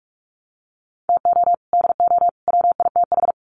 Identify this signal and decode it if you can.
Morse code, heard as a keyed tone, about 44 wpm, 700 Hz. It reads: TO DO WITH
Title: Sets of 3 Words 45wpm